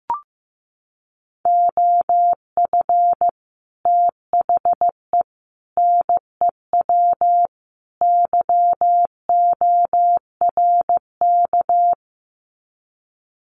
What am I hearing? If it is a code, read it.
Morse code, heard as a keyed tone, about 15 wpm, 700 Hz. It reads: OF THE NEW YORK
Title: Sets of 4 Words 15wpm